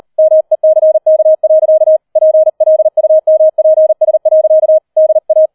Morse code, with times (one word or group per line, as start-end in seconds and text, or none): 0.2-2.0 MECK.
2.2-4.8 PLUMPS.
5.0-5.5 DA